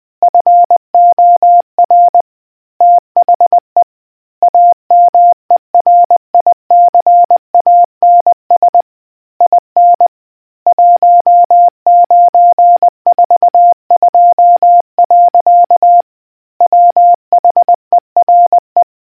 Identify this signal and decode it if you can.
Morse code, heard as a keyed tone, about 20 wpm, 700 Hz. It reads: FOR THE AMERICANS IN 1942. WHERE